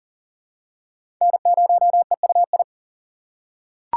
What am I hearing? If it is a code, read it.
Morse code, heard as a keyed tone, about 40 wpm, 700 Hz. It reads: N0EUI
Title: USA Call Signs - 5 Characters - 2x Character Spacing - 40wpm